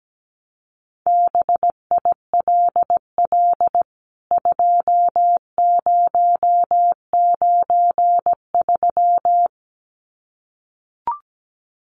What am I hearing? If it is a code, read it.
Morse code, heard as a keyed tone, about 17 wpm, 700 Hz. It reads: BILL 2093
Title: CWT Exchange 17wpm